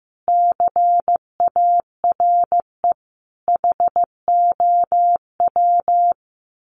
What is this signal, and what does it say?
Morse code, heard as a keyed tone, about 15 wpm, 700 Hz. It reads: CARE HOW
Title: Sets of 5 Words Encore2 15wpm